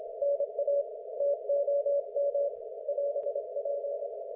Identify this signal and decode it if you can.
Morse code, heard as a keyed tone, about 27 wpm, 570 Hz. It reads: NA TOM <AS>P
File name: Archiv / CUC216 / ok1fhi cul216